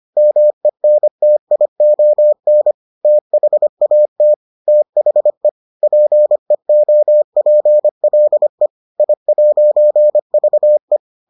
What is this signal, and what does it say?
Morse code, heard as a keyed tone, about 25 wpm, 600 Hz. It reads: MENTION THAT THE PEOPLE I'VE